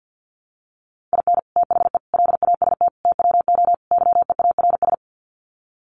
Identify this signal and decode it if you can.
Morse code, heard as a keyed tone, about 48 wpm, 700 Hz. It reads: IN THE LAST TWO YEARS
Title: Sets of 5 Words 50wpm